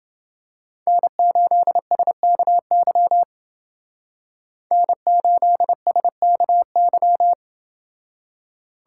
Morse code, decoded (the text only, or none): N8SKY N8SKY